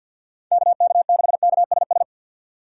Morse code, heard as a keyed tone, about 48 wpm, 700 Hz. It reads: KK6KSS
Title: USA Call Signs - 6 Characters - 6x Character Spacing - 50wpm